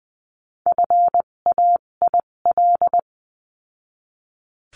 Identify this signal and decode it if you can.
Morse code, heard as a keyed tone, about 20 wpm, spaced as for 17 wpm, 700 Hz. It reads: FAIL
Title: 257 Lesson 42b - Words 14-20wpm